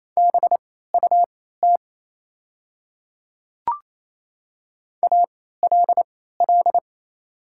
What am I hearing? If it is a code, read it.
Morse code, heard as a keyed tone, about 28 wpm, spaced as for 16 wpm, 700 Hz. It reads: BUT ALL
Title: 3x Character Spacing - Top 200 Words - 28wpm